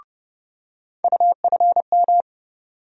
Morse code, heard as a keyed tone, about 30 wpm, 700 Hz. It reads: UFM
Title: USA Call Sign Postfixes - 3 Letters 30wpm